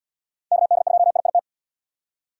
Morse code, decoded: KD2IIT

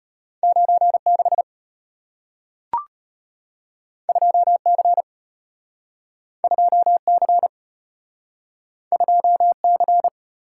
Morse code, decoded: 96 2C 2C 2C

